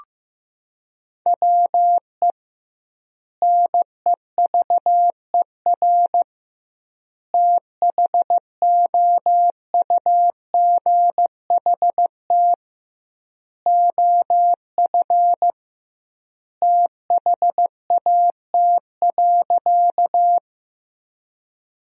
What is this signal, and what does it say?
Morse code, heard as a keyed tone, about 15 wpm, 700 Hz. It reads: WE NEVER THOUGHT OF THAT.
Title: 2x Word-Spacing - Sentences from Top 300 Words - 15wpm